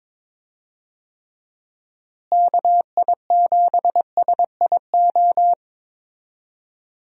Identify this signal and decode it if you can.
Morse code, heard as a keyed tone, about 21 wpm, 700 Hz. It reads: KI7SIO
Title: USA Call Signs - 6 Characters - Repeated 2x - 22wpm